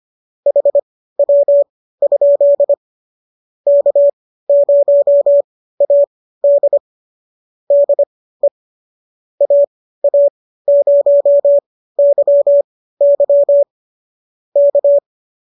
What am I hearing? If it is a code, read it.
Morse code, heard as a keyed tone, about 25 wpm, spaced as for 15 wpm, 570 Hz. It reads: HW? K0AD DE AA0YY K